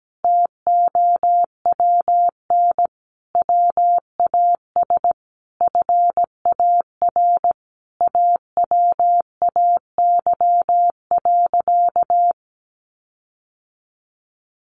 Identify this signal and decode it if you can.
Morse code, heard as a keyed tone, about 17 wpm, 700 Hz. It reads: TOWN WAS FAR AWAY.
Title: Sentences from Top 1500 Words 17wpm